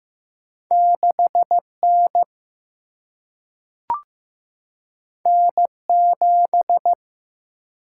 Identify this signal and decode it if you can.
Morse code, heard as a keyed tone, about 15 wpm, 700 Hz. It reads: 6 N N 7
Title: Single-Letter Single-Number 15wpm